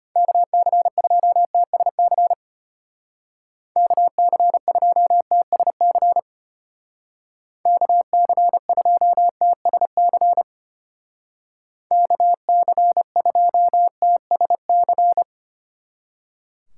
Morse code, heard as a keyed tone, about 32 wpm, 700 Hz. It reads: KC2TSC KC2TSC KC2TSC KC2TSC